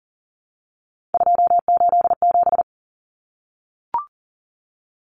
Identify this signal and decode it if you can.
Morse code, heard as a keyed tone, about 40 wpm, 700 Hz. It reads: 287
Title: Numbers - 3-Digit 40wpm